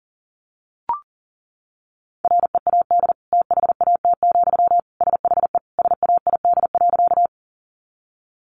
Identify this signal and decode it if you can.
Morse code, heard as a keyed tone, about 40 wpm, 700 Hz. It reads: READ THAT, SHE SAID.
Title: Sentences from Top 200 Words 40wpm